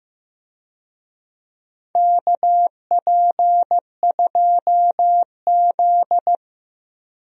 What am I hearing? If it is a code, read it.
Morse code, heard as a keyed tone, about 15 wpm, 700 Hz. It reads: KP2Z
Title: DXCC-100 Call Signs - 4 Characters - 15wpm